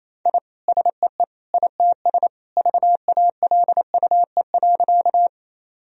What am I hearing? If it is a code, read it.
Morse code, heard as a keyed tone, about 28 wpm, 700 Hz. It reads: I SEE ITS VALUE.
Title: Sentences from Top 1000 Words - Review Part 3 - 28wpm